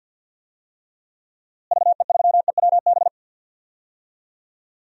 Morse code, heard as a keyed tone, about 52 wpm, 700 Hz. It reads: VE3EWB
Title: DXCC-100 Call Signs - 6 Characters - Repeated 2x - 50wpm